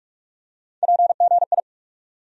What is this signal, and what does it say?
Morse code, heard as a keyed tone, about 46 wpm, 700 Hz. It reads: PGI
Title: USA Call Sign Postfixes - 3 Letters 45wpm